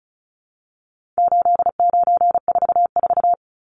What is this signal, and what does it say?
Morse code, heard as a keyed tone, about 34 wpm, 700 Hz. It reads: 8944
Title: Numbers - 4-Digit 35wpm